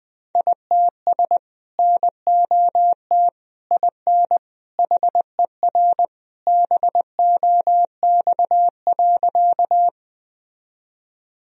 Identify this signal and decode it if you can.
Morse code, heard as a keyed tone, about 20 wpm, 700 Hz. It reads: ITS NOT IN HER BOX.
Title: Sentences from Top 1000 Words - Review Part 2 - 20wpm